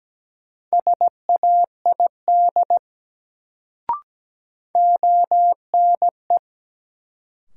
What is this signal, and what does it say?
Morse code, heard as a keyed tone, about 17 wpm, 700 Hz. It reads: SAID ONE
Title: Sets of 2 Words - New Word Drill - 17wpm